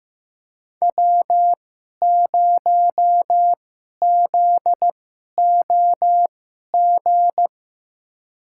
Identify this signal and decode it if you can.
Morse code, heard as a keyed tone, about 15 wpm, spaced as for 11 wpm, 700 Hz. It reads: W0ZOG
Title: USA Call Signs - 5 Characters - 2x Character Spacing - 15wpm